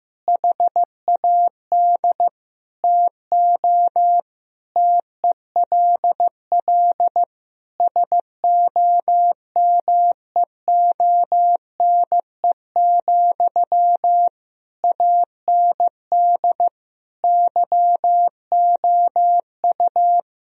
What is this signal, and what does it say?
Morse code, heard as a keyed tone, about 16 wpm, 700 Hz. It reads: HAD TO TELL SOMEONE, AND YOU
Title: Sentences from Top 1000 Words - Review Part 4 - 15wpm